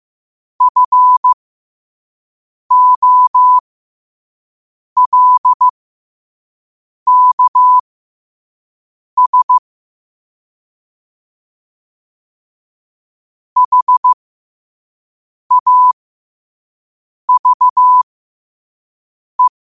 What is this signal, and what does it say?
Morse code, heard as a keyed tone, about 15 wpm, spaced as for 5 wpm, 1000 Hz. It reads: FOLKS HAVE
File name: qotdfiles / 5wpm